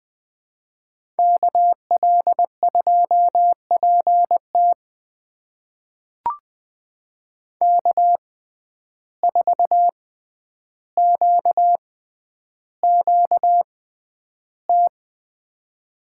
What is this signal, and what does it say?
Morse code, heard as a keyed tone, about 20 wpm, spaced as for 10 wpm, 700 Hz. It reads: KL2PT K4QQT
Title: USA Call Signs - 5 Characters - 6x Character Spacing - 20wpm